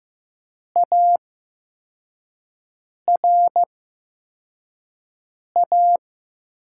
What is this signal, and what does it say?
Morse code, heard as a keyed tone, about 15 wpm, spaced as for 4 wpm, 700 Hz. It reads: ARA
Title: USA Call Signs - 8x Character Spacing - 15wpm